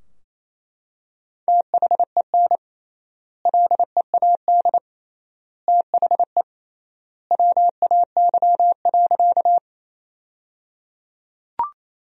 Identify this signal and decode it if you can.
Morse code, heard as a keyed tone, about 28 wpm, 700 Hz. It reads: THEN LEAD THE WAY.
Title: 3x Word-Spacing - Sentences from Top 500 Words - 28wpm